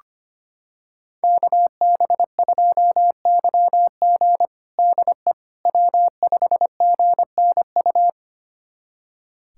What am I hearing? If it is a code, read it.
Morse code, heard as a keyed tone, about 25 wpm, 700 Hz. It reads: KB2YG DE W5GNU
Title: USA Callsign DE Callsign - 5 Characters - 25wpm